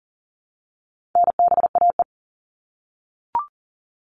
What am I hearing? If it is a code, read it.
Morse code, heard as a keyed tone, about 40 wpm, 700 Hz. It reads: NBAE
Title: All Cut Numbers - 4-Digit 40wpm